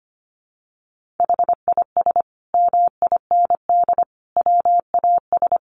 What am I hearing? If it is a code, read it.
Morse code, heard as a keyed tone, about 25 wpm, 700 Hz. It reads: HIS MIND WAS
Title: Sentences from Top 1500 Words 25wpm